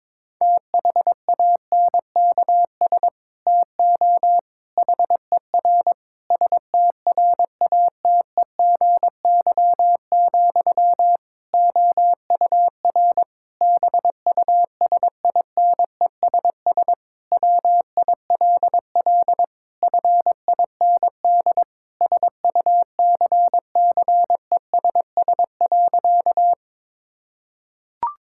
Morse code, heard as a keyed tone, about 22 wpm, 700 Hz. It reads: THANKS TO HER STRATEGY, OUR BUSINESS WILL FIND SUCCESS.
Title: Sentences from Top 1000 Words - Review Part 5 - 22wpm